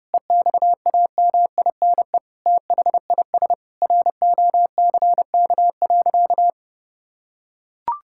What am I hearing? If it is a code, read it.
Morse code, heard as a keyed tone, about 30 wpm, 700 Hz. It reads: EXAMINE THIS ROCK.
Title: Sentences from Top 1600 Words 30wpm